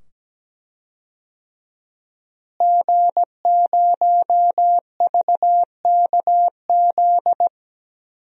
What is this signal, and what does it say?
Morse code, heard as a keyed tone, about 17 wpm, 700 Hz. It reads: G0VKZ
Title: DXCC-100 Call Signs - 5 Characters - Repeated 2x - 17wpm